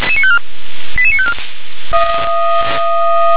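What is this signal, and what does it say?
Five-tone selective call (ZVEI), heard as quick runs of five tones, tones of about 70 ms. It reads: 88954
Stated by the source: ZVEI five-tone selective call received on an SDR.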